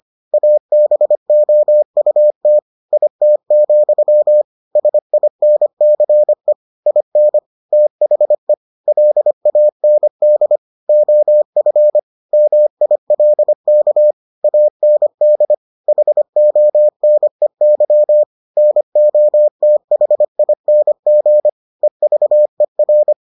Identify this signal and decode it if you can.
Morse code, heard as a keyed tone, about 25 wpm, 600 Hz. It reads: ABOUT IT, SINCE IN THE LAND OF MILK AND HONEY NOTHING EVER